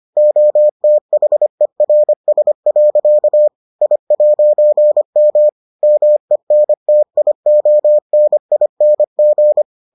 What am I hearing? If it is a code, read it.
Morse code, heard as a keyed tone, about 25 wpm, 600 Hz. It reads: OTHERS. I'M MENTIONING